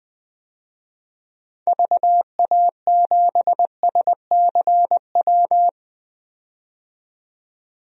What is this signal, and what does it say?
Morse code, heard as a keyed tone, about 20 wpm, 700 Hz. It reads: VA7SCW
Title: DXCC-100 Call Signs - 6 Characters - 20wpm